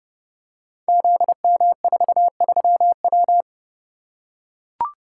Morse code, heard as a keyed tone, about 30 wpm, 700 Hz. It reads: ZM43W
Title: Code Groups - 5 Letter:Numbers - 30wpm